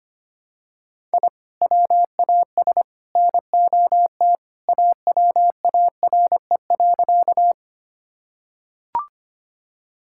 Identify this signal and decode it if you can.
Morse code, heard as a keyed tone, about 25 wpm, 700 Hz. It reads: I WAS NOT AWARE.
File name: practice / 25wpm / Sentences with 3-Word Phrases - 25wpm